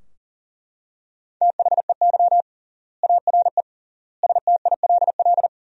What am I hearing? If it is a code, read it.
Morse code, heard as a keyed tone, about 37 wpm, 700 Hz. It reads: THEY ARE STILL
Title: 3x Word-Spacing - Sentences from Top 400 Words - 40wpm